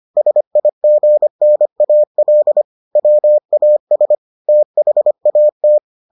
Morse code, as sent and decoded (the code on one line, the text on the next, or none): ... .. --. -. .- .-.. / .-- .- ... / - .... .- -
SIGNAL WAS THAT